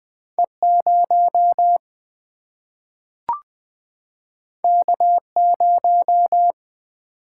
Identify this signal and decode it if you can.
Morse code, heard as a keyed tone, about 20 wpm, 700 Hz. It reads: E0 K0
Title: Single-Letter Single-Number 20wpm